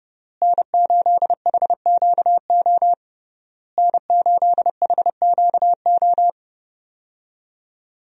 Morse code, heard as a keyed tone, about 30 wpm, 700 Hz. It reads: N8HQO N8HQO
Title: USA Call Signs - 5 Characters - Repeated 3x - 30wpm